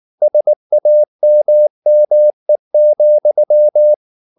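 Morse code, heard as a keyed tone, about 19 wpm, 600 Hz. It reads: SAMME,
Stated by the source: Slow (QRS) Morse practice at 18 wpm, CW med Gnister.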